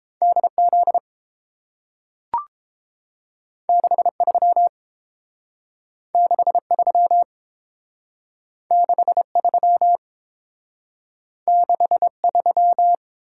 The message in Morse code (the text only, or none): DZ 63 63 63 63